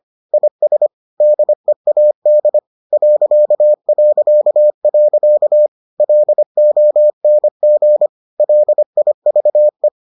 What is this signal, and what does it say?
Morse code, heard as a keyed tone, about 25 wpm, 600 Hz. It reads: IS DEAD ... LONG LIVE